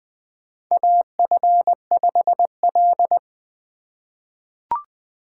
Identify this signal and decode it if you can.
Morse code, heard as a keyed tone, about 21 wpm, 700 Hz. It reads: AF5L
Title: USA Call Signs - 4 Characters - 3x Character Spacing - 20wpm